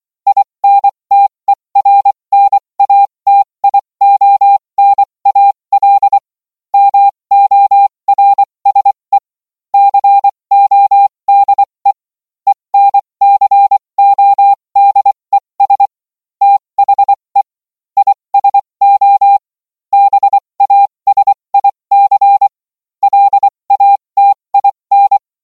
Morse code, heard as a keyed tone, about 24 wpm, spaced as for 20 wpm, 800 Hz. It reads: INTERNATIONAL MORSE CODE ENCODES THE ISO BASIC LATIN